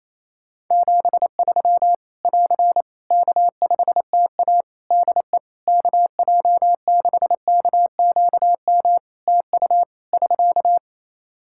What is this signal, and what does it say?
Morse code, heard as a keyed tone, about 28 wpm, 700 Hz. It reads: 73 <AR> K5TA DE KJ6KQM TU <SK>
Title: Practice QSO - Exchange 4 - Ending - 28wpm